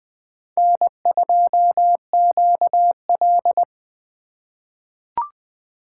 Morse code, heard as a keyed tone, about 20 wpm, 700 Hz. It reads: N2QL
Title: USA Call Signs - 4 Characters - 3x Character Spacing - 20wpm